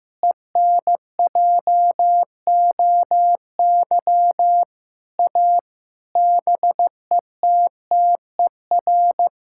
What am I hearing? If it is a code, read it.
Morse code, heard as a keyed tone, about 15 wpm, 700 Hz. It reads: ENJOY A BETTER
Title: Sentences from Top 1000 Words - Review Part 5 - 15wpm